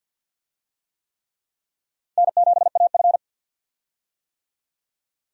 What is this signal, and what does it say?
Morse code, heard as a keyed tone, about 51 wpm, 700 Hz. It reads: N7AF
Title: USA Call Signs - 4 Characters - 50wpm